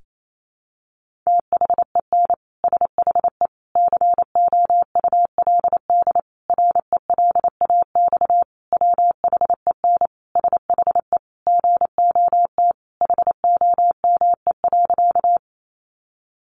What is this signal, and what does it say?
Morse code, heard as a keyed tone, about 28 wpm, 700 Hz. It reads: THEN SHE COULD RELAX WHEN SHE GOT HOME.
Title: Sentences from Top 1400 Words 28wpm